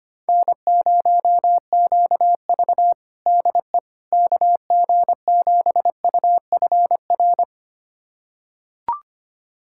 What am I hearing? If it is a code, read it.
Morse code, heard as a keyed tone, about 25 wpm, 700 Hz. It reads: N0QV DE KG7UFR
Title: USA Callsign DE Callsign 25wpm